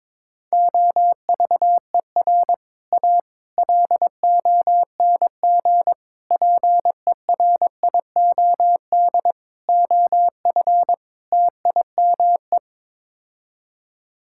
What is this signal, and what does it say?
Morse code, heard as a keyed tone, about 22 wpm, 700 Hz. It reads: OVER A LONG PERIOD OF TIME
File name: practice / 22wpm / Sets of 6 Words - 22wpm